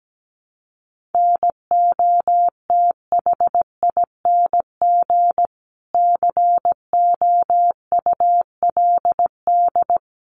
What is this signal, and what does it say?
Morse code, heard as a keyed tone, about 17 wpm, 700 Hz. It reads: NOTHING COULD
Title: Sentences from Top 1300 Words 17wpm